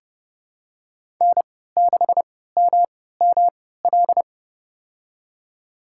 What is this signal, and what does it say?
Morse code, heard as a keyed tone, about 30 wpm, spaced as for 17 wpm, 700 Hz. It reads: N6MML
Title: USA Call Signs - 5 Characters - 3x Character Spacing - 30wpm